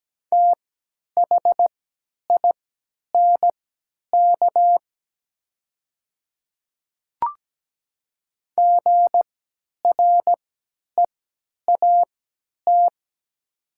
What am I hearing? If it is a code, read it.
Morse code, heard as a keyed tone, about 17 wpm, spaced as for 10 wpm, 700 Hz. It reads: THINK GREAT
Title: 3x Character Spacing - Top 200 Words - 17wpm